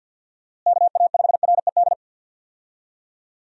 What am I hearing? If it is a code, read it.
Morse code, heard as a keyed tone, about 46 wpm, 700 Hz. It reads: KA5RED
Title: USA Call Signs - 6 Characters - Repeated 3x - 50wpm